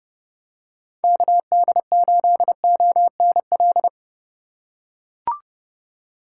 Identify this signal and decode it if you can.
Morse code, heard as a keyed tone, about 32 wpm, 700 Hz. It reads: KD8ONL
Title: USA Call Signs - 6 Characters - 3x Character Spacing - 30wpm